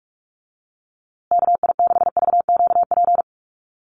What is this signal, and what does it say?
Morse code, heard as a keyed tone, about 44 wpm, 700 Hz. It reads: KI6VQP
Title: USA Call Signs Repeated 3x 45wpm